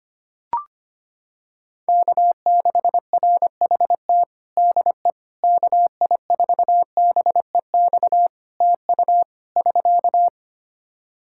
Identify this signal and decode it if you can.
Morse code, heard as a keyed tone, about 25 wpm, 700 Hz. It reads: K6RHT DE KI4BEX TU <SK>
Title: K1USN SST QSO Practice 25wpm